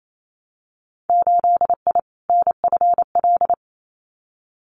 Morse code, heard as a keyed tone, about 28 wpm, 690 Hz. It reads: 8I NFL
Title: Winter Field Day Exchanges Encore 28wpm